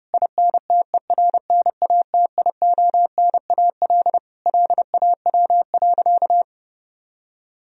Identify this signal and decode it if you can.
Morse code, heard as a keyed tone, about 30 wpm, 700 Hz. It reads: INTERNATIONAL LAW.
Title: Sentences from Top 1000 Words - Review Part 4 - 30wpm